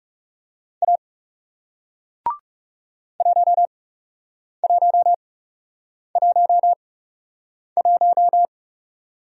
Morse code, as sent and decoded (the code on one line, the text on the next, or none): .- / .---- .---- .---- .----
A 1111